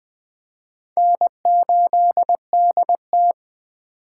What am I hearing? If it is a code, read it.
Morse code, heard as a keyed tone, about 19 wpm, 700 Hz. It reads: N8DT